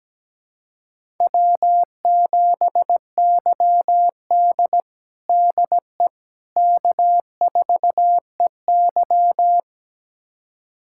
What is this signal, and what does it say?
Morse code, heard as a keyed tone, about 17 wpm, 700 Hz. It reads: W7YD DE K4EY
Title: USA Callsign DE Callsign - 4 Characters - 17wpm